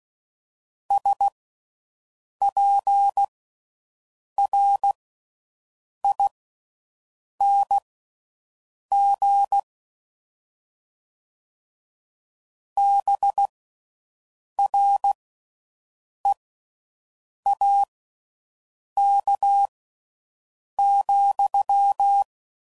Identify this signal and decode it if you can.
Morse code, heard as a keyed tone, about 16 wpm, spaced as for 6 wpm, 800 Hz. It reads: SPRING BREAK,